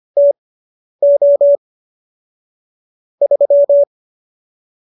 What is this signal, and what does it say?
Morse code, heard as a keyed tone, about 25 wpm, spaced as for 10 wpm, 570 Hz. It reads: TO 3